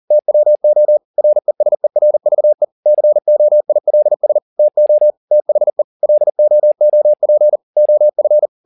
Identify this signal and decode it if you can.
Morse code, heard as a keyed tone, about 40 wpm, 600 Hz. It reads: TWO RESERVE COILS TO THE LOOP OF